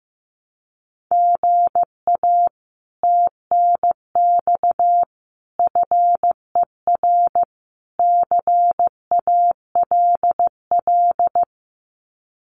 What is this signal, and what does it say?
Morse code, heard as a keyed tone, about 15 wpm, 700 Hz. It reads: GA TNX FER CALL
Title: Common QSO Phrases Encore 15wpm